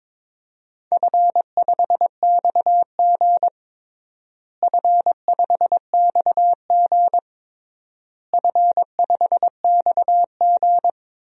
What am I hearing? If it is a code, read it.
Morse code, heard as a keyed tone, about 22 wpm, 700 Hz. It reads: F5XG F5XG F5XG
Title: DXCC-100 Call Signs - 4 Characters - Repeated 3x - 22wpm